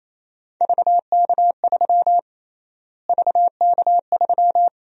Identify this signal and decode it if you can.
Morse code, heard as a keyed tone, about 27 wpm, 700 Hz. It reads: VK3 VK3
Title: USA Call Sign Prefixes - Repeated 2x - 28wpm